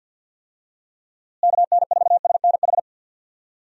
Morse code, heard as a keyed tone, about 48 wpm, 700 Hz. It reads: KN4SNH
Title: USA Call Signs - 6 Characters - Repeated 2x - 50wpm